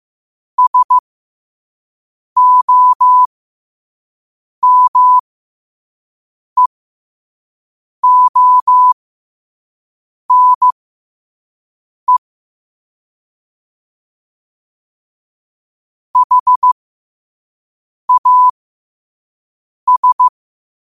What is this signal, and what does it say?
Morse code, heard as a keyed tone, about 15 wpm, spaced as for 5 wpm, 1000 Hz. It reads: SOMEONE HAS
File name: qotdfiles / 5wpm